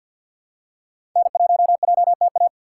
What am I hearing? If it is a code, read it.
Morse code, heard as a keyed tone, about 52 wpm, 700 Hz. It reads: N1JTA
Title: USA Call Signs - 6x Character Spacing - 50wpm